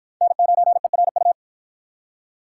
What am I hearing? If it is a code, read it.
Morse code, heard as a keyed tone, about 48 wpm, 700 Hz. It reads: N9ERU